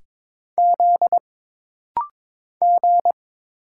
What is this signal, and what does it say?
Morse code, heard as a keyed tone, about 22 wpm, 700 Hz. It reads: Z G
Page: This rapid-fire timing is geared to push your ICR (Instant Character Recognition) skills.